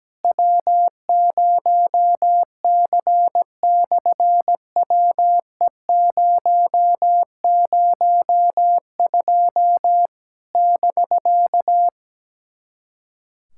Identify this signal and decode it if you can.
Morse code, heard as a keyed tone, about 17 wpm, 700 Hz. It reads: W0C/WE002 <BK>
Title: SOTA QSO Practice 17wpm